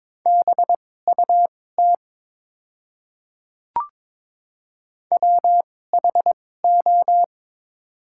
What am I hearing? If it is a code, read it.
Morse code, heard as a keyed tone, about 22 wpm, spaced as for 16 wpm, 700 Hz. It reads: BUT WHO
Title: 2x Character Spacing - Top 300 Words - 22wpm